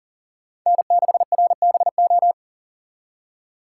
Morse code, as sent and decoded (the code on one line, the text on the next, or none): -. -.... .-. -... ---
N6RBO